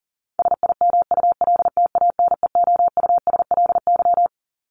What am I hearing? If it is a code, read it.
Morse code, heard as a keyed tone, about 40 wpm, 700 Hz. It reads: SIMULTANEOUSLY
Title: LY Suffix - Top 100 Words with Base Words 40wpm